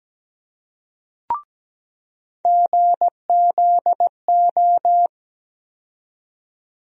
Morse code, heard as a keyed tone, about 17 wpm, 700 Hz. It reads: GZO
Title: USA Call Sign Postfixes - 3 Letters 17wpm